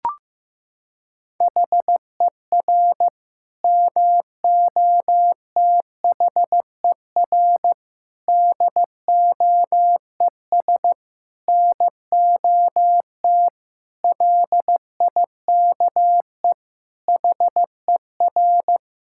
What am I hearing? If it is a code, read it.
Morse code, heard as a keyed tone, about 15 wpm, 700 Hz. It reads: HER MOTHER DOES NOT LIKE HER
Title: Sentences from Top 1000 Words - Review Part 5 - 15wpm